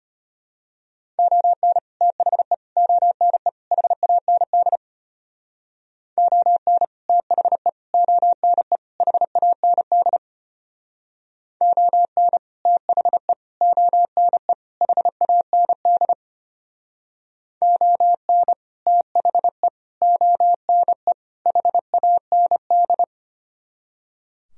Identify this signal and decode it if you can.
Morse code, heard as a keyed tone, about 32 wpm, 700 Hz. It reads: ON THE ONE HAND ON THE ONE HAND ON THE ONE HAND ON THE ONE HAND